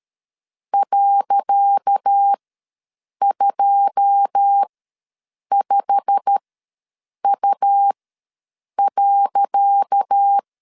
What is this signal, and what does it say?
Morse code, heard as a keyed tone, about 13 wpm, spaced as for 7 wpm, 800 Hz. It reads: .25U.
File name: LEZIONI / 10 SIGNS / 10C DOT ws 12wpm 3548